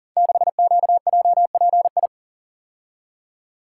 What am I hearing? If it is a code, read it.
Morse code, heard as a keyed tone, about 41 wpm, 700 Hz. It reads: 6QJPI
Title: Code Groups - 5 Letter:Numbers - 40wpm